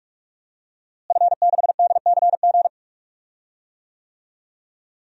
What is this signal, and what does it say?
Morse code, heard as a keyed tone, about 41 wpm, 700 Hz. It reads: F6DCG
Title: DXCC-100 Call Signs - 5 Characters - Repeated 2x - 45wpm